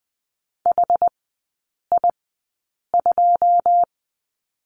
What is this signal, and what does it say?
Morse code, heard as a keyed tone, about 20 wpm, spaced as for 8 wpm, 690 Hz. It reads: HI2
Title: 081 Lesson 14e - Number 2 - Three Characters 20wpm